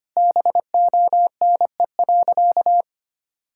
Morse code, heard as a keyed tone, about 25 wpm, 700 Hz. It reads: BONE.
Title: Sentences from Top 1600 Words 25wpm